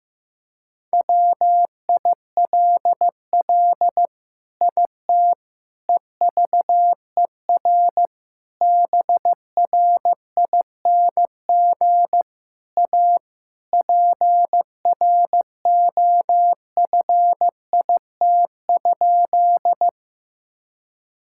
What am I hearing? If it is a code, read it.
Morse code, heard as a keyed tone, about 15 wpm, 700 Hz. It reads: WILL IT EVER BRING A PROFIT?
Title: Sentences from Top 1000 Words - Review Part 5 - 15wpm